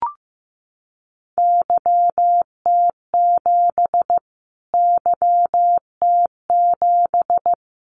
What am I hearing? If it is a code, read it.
Morse code, heard as a keyed tone, about 15 wpm, 700 Hz. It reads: YT7 YT7
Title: Callsign Prefixes for Top 25 DXCC - Repeated 3x - 15wpm